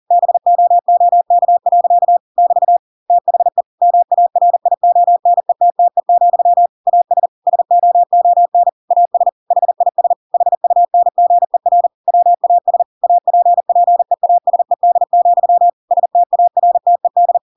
Morse code, heard as a keyed tone, about 40 wpm, 700 Hz. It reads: BOOK. <BT> THE MARIONETTE, AS SOON AS HIS HUNGER WAS APPEASED, STARTED